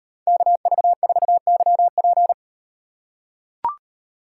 Morse code, heard as a keyed tone, about 36 wpm, 700 Hz. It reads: KV4YP